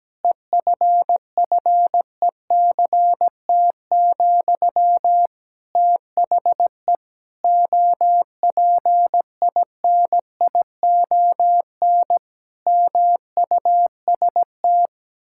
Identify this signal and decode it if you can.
Morse code, heard as a keyed tone, about 17 wpm, 700 Hz. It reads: EFFECT, THE OPINION MUST
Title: Sentences from Top 1600 Words 17wpm